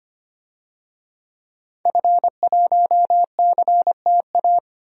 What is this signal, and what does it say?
Morse code, heard as a keyed tone, about 25 wpm, 700 Hz. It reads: F1CTA
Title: DXCC-100 Call Signs - 5 Characters - Repeated 2x - 25wpm